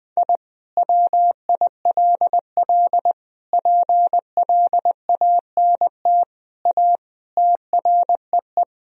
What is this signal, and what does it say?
Morse code, heard as a keyed tone, about 20 wpm, 700 Hz. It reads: I WILL PLANT A TREE
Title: Sentences from Top 1000 Words - Review Part 1 - 20wpm